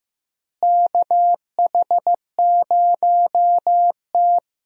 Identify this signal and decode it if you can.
Morse code, heard as a keyed tone, about 15 wpm, 700 Hz. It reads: KH0T
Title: USA Call Signs - 4 Characters - Repeated 3x - 15wpm